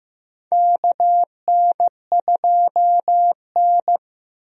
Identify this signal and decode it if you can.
Morse code, heard as a keyed tone, about 16 wpm, 700 Hz. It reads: KN2N
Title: USA Call Signs - 4 Characters - Repeated 3x - 15wpm